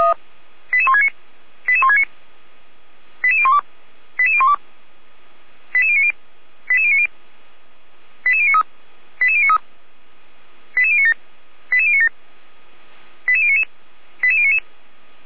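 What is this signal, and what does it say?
Five-tone selective call (ZVEI), heard as quick runs of five tones, tones of about 70 ms. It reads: 88169 88169 88012 88012 88090 88090 88093 88093 88097 88097 88099 88099
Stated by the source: Five-tone ZVEI selective call received on an SDR, from the fire brigade Feuerwehr Orlamünde.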